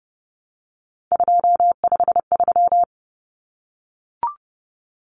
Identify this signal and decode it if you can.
Morse code, heard as a keyed tone, about 30 wpm, 700 Hz. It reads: 253
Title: Numbers - 3-Digit 30wpm